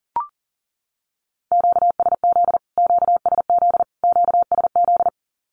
Morse code, heard as a keyed tone, about 40 wpm, 700 Hz. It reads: QSZ QSZ QSZ
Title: Q-Codes Repeated 3x 40wpm